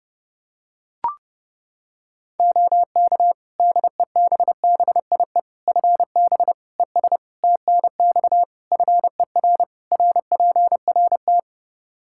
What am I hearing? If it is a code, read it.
Morse code, heard as a keyed tone, about 30 wpm, 700 Hz. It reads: OK DEBBIE FB ES TNX FER RPRT
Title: Practice QSO - Exchange 2 - Info - 30wpm